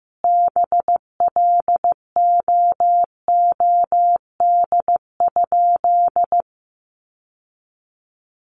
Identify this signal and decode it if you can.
Morse code, heard as a keyed tone, about 15 wpm, 700 Hz. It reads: BLOOD?
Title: Sentences from Top 1400 Words 15wpm